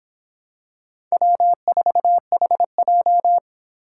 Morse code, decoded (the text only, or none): W4HJ